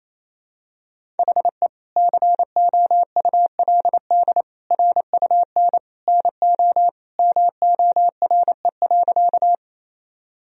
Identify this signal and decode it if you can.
Morse code, heard as a keyed tone, about 28 wpm, 700 Hz. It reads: HE COULD RUN NO MORE.
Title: Sentences from Top 1000 Words - Review Part 1 - 28wpm